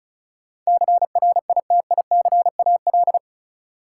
Morse code, heard as a keyed tone, about 36 wpm, 700 Hz. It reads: CRITICAL
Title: Sentences from Top 1100 Words - New Word Drill Alt - 35wpm